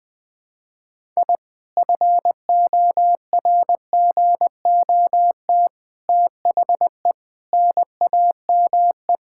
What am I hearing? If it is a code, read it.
Morse code, heard as a keyed tone, about 20 wpm, 700 Hz. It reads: I FORGOT THE NAME
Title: Sentences with 4-Word Phrases - 20wpm